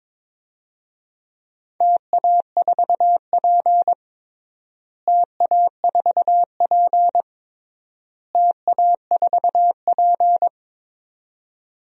Morse code, heard as a keyed tone, about 22 wpm, 700 Hz. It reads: TA4P TA4P TA4P
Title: DXCC-100 Call Signs - 4 Characters - Repeated 3x - 22wpm